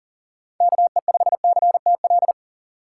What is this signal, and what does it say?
Morse code, heard as a keyed tone, about 40 wpm, 700 Hz. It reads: KE5CTL
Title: USA Call Signs - 6 Characters - 2x Character Spacing - 40wpm